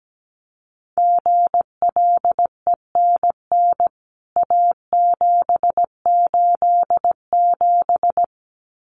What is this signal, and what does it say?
Morse code, heard as a keyed tone, about 17 wpm, 700 Hz. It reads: GLENN A787
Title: CWT Exchange 17wpm